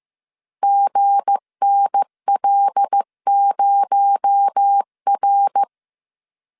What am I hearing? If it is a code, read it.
Morse code, heard as a keyed tone, about 15 wpm, 800 Hz. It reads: GNL0R
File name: CW / DET / 1 9 1500C S C 14wpm 3245